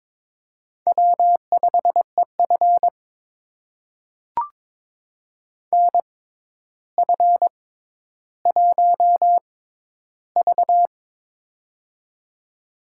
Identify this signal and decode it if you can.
Morse code, heard as a keyed tone, about 22 wpm, spaced as for 11 wpm, 700 Hz. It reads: W5EF NF1V
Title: USA Call Signs - 4 Characters - 6x Character Spacing - 22wpm